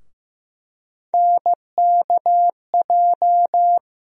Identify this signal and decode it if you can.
Morse code, heard as a keyed tone, about 15 wpm, 700 Hz. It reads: NKJ
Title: USA Call Sign Postfixes - 3 Letters 15wpm